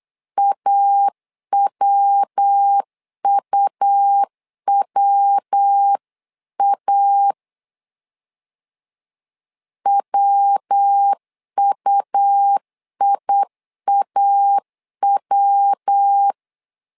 Morse code, heard as a keyed tone, about 8 wpm, 800 Hz. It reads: AWUWA WUIAW
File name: LEZIONI / 3 MWRO / 3 MW 8wpm 2501